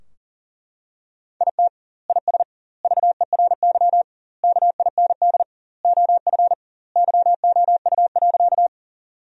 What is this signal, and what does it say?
Morse code, heard as a keyed tone, about 38 wpm, 700 Hz. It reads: IT IS VERY KIND OF YOU.
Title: 2x Word-Spacing - Sentences from Top 200 Words - 40wpm